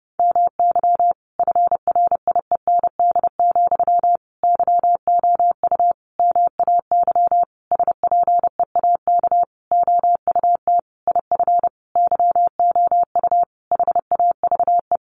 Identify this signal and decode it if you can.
Morse code, heard as a keyed tone, about 30 wpm, 700 Hz. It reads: MY FRIEND, YOU MAY SPEAK OUT IF YOU HAVE
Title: Sentences from Top 1300 Words 30wpm